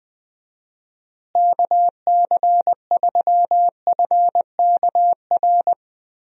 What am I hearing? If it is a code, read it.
Morse code, heard as a keyed tone, about 20 wpm, 700 Hz. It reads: KC3FKR
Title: USA Call Signs - 6 Characters - Repeated 2x - 20wpm